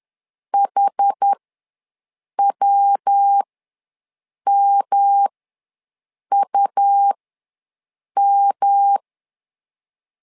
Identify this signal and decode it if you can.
Morse code, heard as a keyed tone, about 10 wpm, spaced as for 6 wpm, 800 Hz. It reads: HWMUM
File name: LEZIONI / 3 MWRO / 3 MW ws 10wpm 2213